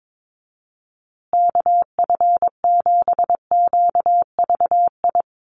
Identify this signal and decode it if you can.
Morse code, heard as a keyed tone, about 22 wpm, 700 Hz. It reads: KF7QVI
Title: USA Call Signs Repeated 3x 22wpm